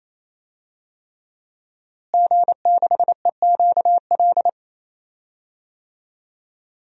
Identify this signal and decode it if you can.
Morse code, heard as a keyed tone, about 28 wpm, 700 Hz. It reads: G6EQL
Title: DXCC-100 Call Signs - 5 Characters - Repeated 3x - 28wpm